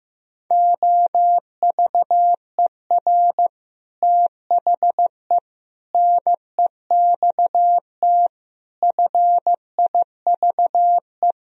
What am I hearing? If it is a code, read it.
Morse code, heard as a keyed tone, about 15 wpm, 700 Hz. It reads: OVER THE NEXT FIVE
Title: Speed-Racing - Sets of 5 Words 15wpm